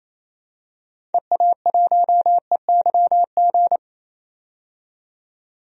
Morse code, heard as a keyed tone, about 28 wpm, 700 Hz. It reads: EA1EYG
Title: DXCC-100 Call Signs - 6 Characters - 28wpm